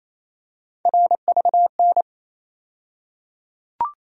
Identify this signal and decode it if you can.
Morse code, heard as a keyed tone, about 28 wpm, 700 Hz. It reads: RVN
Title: USA Call Sign Postfixes - 3 Letters 28wpm